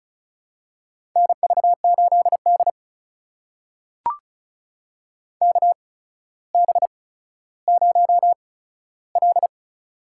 Morse code, heard as a keyed tone, about 36 wpm, spaced as for 15 wpm, 700 Hz. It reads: NV8D KB0L
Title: USA Call Signs - 4 Characters - 8x Character Spacing - 35wpm